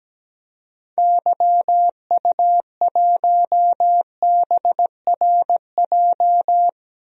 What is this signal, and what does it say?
Morse code, heard as a keyed tone, about 17 wpm, 700 Hz. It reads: YU1BRJ
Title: DXCC-100 Call Signs - 6 Characters - Repeated 2x - 17wpm